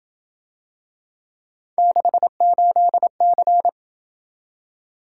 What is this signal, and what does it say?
Morse code, heard as a keyed tone, about 27 wpm, 700 Hz. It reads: 68C